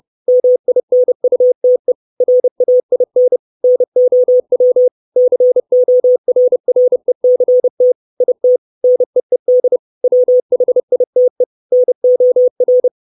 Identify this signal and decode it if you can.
Morse code, heard as a keyed tone, about 30 wpm, 500 Hz. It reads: MINUTE RAIN NOW CORRECT IT NEED WHITE NOR
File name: 632 Words 30WPM